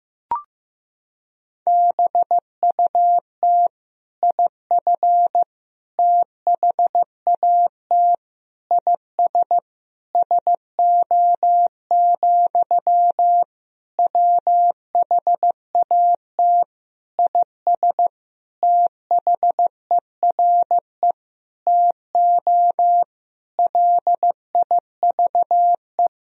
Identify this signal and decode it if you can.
Morse code, heard as a keyed tone, about 15 wpm, 700 Hz. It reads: BUT IF THAT IS SO, WHAT IS THERE TO LIVE
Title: Sentences from Top 200 Words and Abbreviations 15wpm